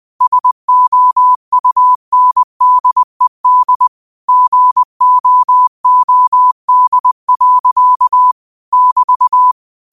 Morse code, heard as a keyed tone, about 20 wpm, 1000 Hz. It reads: SOUNDED GOOD. <BT>